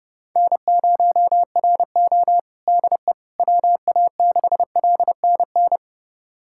Morse code, heard as a keyed tone, about 30 wpm, 700 Hz. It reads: N0RO DE WA6LNN
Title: USA Callsign DE Callsign 30wpm